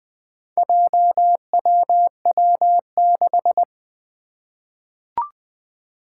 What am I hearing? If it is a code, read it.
Morse code, heard as a keyed tone, about 20 wpm, 700 Hz. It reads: JWW6
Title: Four Challenging Letter-Number Combinations 20wpm